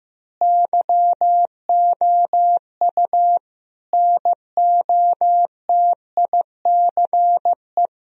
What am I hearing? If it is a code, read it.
Morse code, heard as a keyed tone, about 16 wpm, 700 Hz. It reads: YOU NOTICE
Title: Sentences from Top 1000 Words - Review Part 2 - 15wpm